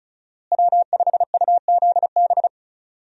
Morse code, heard as a keyed tone, about 36 wpm, 700 Hz. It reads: W5UZB
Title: USA Call Signs - 5 Characters - Repeated 3x - 35wpm